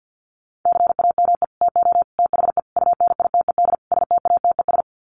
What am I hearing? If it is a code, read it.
Morse code, heard as a keyed tone, about 50 wpm, 700 Hz. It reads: CAME TO THE UNITED STATES
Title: Sets of 5 Words Encore2 50wpm